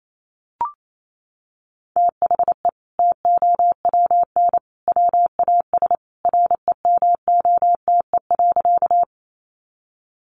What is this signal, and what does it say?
Morse code, heard as a keyed tone, about 28 wpm, 700 Hz. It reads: THE TOWN WAS REMOTE.
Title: Sentences from Top 1300 Words 28wpm